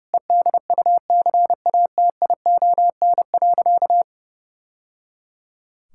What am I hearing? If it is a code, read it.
Morse code, heard as a keyed tone, about 30 wpm, 700 Hz. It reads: EDUCATION.
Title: Sentences from Top 1000 Words - Review Part 6 - 30wpm